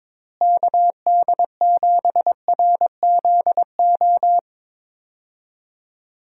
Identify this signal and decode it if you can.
Morse code, heard as a keyed tone, about 22 wpm, 700 Hz. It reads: KD7RZO
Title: USA Call Signs - 6 Characters - Repeated 3x - 22wpm